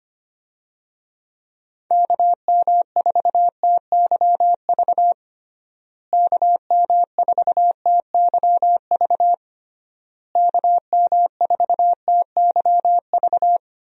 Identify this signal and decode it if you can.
Morse code, heard as a keyed tone, about 25 wpm, 700 Hz. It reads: KM4TYV KM4TYV KM4TYV
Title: USA Call Signs - 6 Characters - Repeated 3x - 25wpm